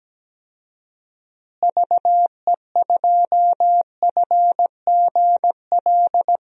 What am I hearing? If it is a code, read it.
Morse code, heard as a keyed tone, about 17 wpm, 700 Hz. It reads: VE2FGL
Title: DXCC-100 Call Signs - 6 Characters - Repeated 3x - 17wpm